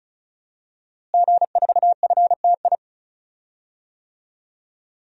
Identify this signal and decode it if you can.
Morse code, heard as a keyed tone, about 34 wpm, 700 Hz. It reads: G4FTI
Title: DXCC-100 Call Signs - 5 Characters - Repeated 3x - 35wpm